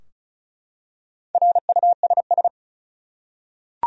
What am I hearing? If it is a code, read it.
Morse code, heard as a keyed tone, about 34 wpm, 700 Hz. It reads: RUSS
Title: USA Common Names - 35wpm